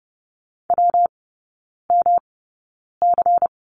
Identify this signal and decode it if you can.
Morse code, heard as a keyed tone, about 30 wpm, spaced as for 9 wpm, 700 Hz. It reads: WMC
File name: Lesson 19e - Letter w - Three Characters 30wpm